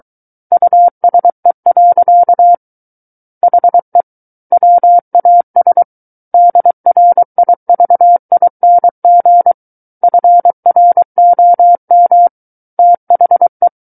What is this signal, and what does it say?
Morse code, heard as a keyed tone, about 23 wpm, 700 Hz. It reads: USE. HE WAS DRIVING FROM THE
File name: Book One / 14-Chapter 14 - 23 WPM